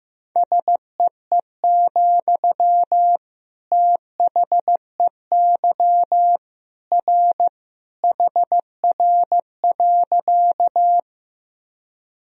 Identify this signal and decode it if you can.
Morse code, heard as a keyed tone, about 16 wpm, 700 Hz. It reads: SEE, THEY R HR.
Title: Sentences from Top 200 Words and Abbreviations 15wpm